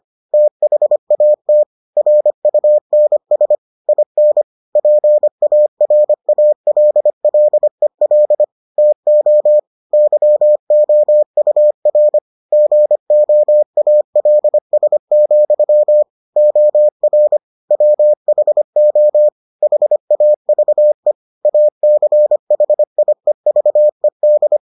Morse code, heard as a keyed tone, about 25 wpm, 600 Hz. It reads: THAT RUNS IN PARALLEL TO YOUR GOALS, OR WHO HAVE ACHIEVED